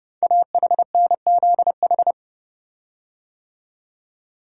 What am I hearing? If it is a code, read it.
Morse code, heard as a keyed tone, about 30 wpm, 700 Hz. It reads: AHNZH
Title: Code Groups - 5 Letters - 30wpm